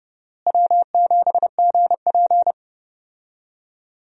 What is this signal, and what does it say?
Morse code, heard as a keyed tone, about 31 wpm, 700 Hz. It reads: W7GP
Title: USA Call Signs - 4 Characters - Repeated 3x - 30wpm